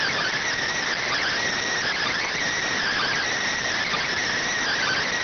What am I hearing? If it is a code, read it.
Emergency Alert System (SAME) tones, heard as harsh buzzing bursts. It is heard from 0.9 to 5.0 s.
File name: findings / CT CORONAL 710mm 10mm scout.gen-35 rx